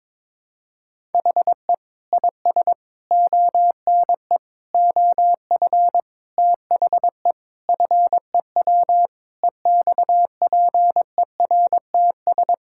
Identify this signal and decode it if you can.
Morse code, heard as a keyed tone, about 21 wpm, 700 Hz. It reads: HE IS ONE OF THE FEW EXPERTS
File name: practice / 22wpm / Sentences with 4-Word Phrases - 22wpm